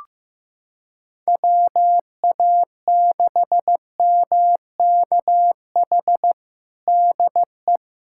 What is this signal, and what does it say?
Morse code, heard as a keyed tone, about 15 wpm, 700 Hz. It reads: WA6MKH DE
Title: Practice QSO - Exchange 4 - Ending - 15wpm